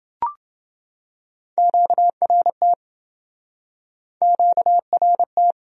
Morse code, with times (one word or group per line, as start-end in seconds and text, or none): 1.6-2.7 QRT
4.2-5.5 QRT